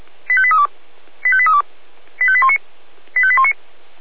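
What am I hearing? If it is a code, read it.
Five-tone selective call (ZVEI), heard as quick runs of five tones, tones of about 70 ms. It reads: 85832 85832 85819 85819
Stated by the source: ZVEI five-tone call received by SDR, audio.f85360.feuerwehr-orlamuende.org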